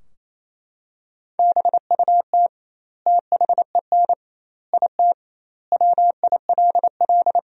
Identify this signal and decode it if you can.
Morse code, heard as a keyed tone, about 28 wpm, 700 Hz. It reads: BUT THEN IT WILL
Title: 2x Word-Spacing - Sentences from Top 300 Words - 28wpm